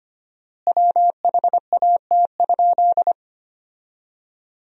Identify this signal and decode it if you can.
Morse code, heard as a keyed tone, about 25 wpm, 700 Hz. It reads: WHAT?
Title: 4x Word-Spacing - Sentences from Top 100 Words - 25wpm